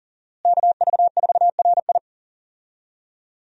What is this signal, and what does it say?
Morse code, heard as a keyed tone, about 40 wpm, 700 Hz. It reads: KV4RI
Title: USA Call Signs - 5 Characters - Repeated 3x - 40wpm